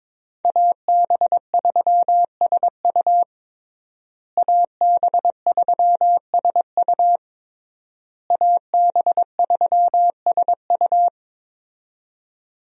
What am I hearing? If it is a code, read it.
Morse code, heard as a keyed tone, about 22 wpm, 700 Hz. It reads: AB3SU AB3SU AB3SU
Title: USA Call Signs - 5 Characters - Repeated 3x - 22wpm